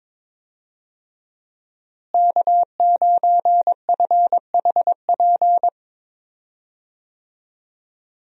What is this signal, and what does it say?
Morse code, heard as a keyed tone, about 22 wpm, 700 Hz. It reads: K9FHP
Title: USA Call Signs - 5 Characters - Repeated 3x - 22wpm